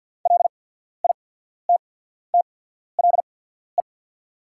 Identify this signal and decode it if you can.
Morse code, heard as a keyed tone, about 48 wpm, spaced as for 14 wpm, 690 Hz. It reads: LITTLE
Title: 8x Character Spacing - Top 200 Words - 50wpm